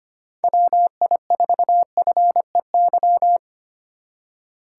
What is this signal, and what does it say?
Morse code, heard as a keyed tone, about 25 wpm, 700 Hz. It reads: WI4FEY
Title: USA Call Signs - 6 Characters - 2x Character Spacing - 25wpm